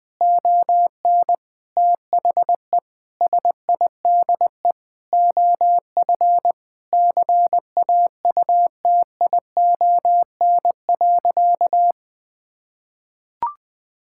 Morse code, heard as a keyed tone, about 21 wpm, 700 Hz. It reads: ON THE SIDE OF CAUTION.
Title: Sentences with 7-Word Phrases - Spoken Beginning - 20wpm